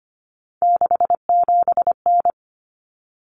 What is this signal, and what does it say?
Morse code, heard as a keyed tone, about 25 wpm, 700 Hz. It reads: 67N
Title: Common Cut Numbers - 3-Digit 25wpm